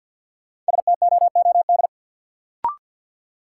Morse code, heard as a keyed tone, about 53 wpm, 700 Hz. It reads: STOOD
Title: Sentences from Top 500 Words - New Word Drill - 50wpm